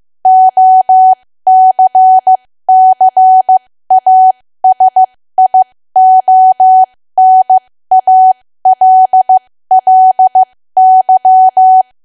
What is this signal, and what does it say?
Morse code, heard as a keyed tone, about 15 wpm, spaced as for 13 wpm, 750 Hz. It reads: OCCASIONALLY
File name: cwa / W1AW-Level2 / 13WPM